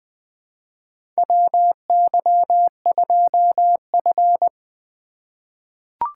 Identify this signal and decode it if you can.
Morse code, heard as a keyed tone, about 20 wpm, 700 Hz. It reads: WY2F
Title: USA Call Signs - 4 Characters - 4x Character Spacing - 20wpm